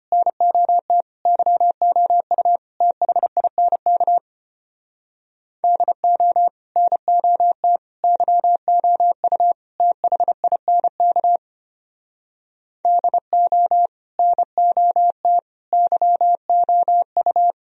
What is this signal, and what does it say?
Morse code, heard as a keyed tone, about 31 wpm, 700 Hz. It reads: NOT YOU THINK DO NOT YOU THINK DO NOT YOU
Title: Speed-Racing - Sets of 4 Words 25wpm